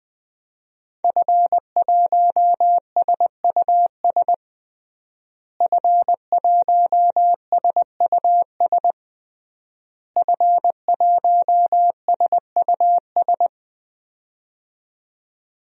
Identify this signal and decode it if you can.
Morse code, heard as a keyed tone, about 20 wpm, 700 Hz. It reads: F1SUS F1SUS F1SUS
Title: DXCC-100 Call Signs - 5 Characters - Repeated 3x - 20wpm